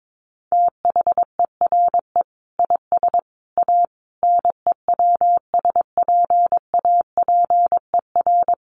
Morse code, heard as a keyed tone, about 22 wpm, 700 Hz. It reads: THERE IS A NEWSPAPER
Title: Sentences from Top 1100 Words 22wpm